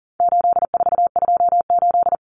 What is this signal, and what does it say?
Morse code, heard as a keyed tone, about 40 wpm, 700 Hz. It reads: 8428
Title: Numbers - 4-Digit 40wpm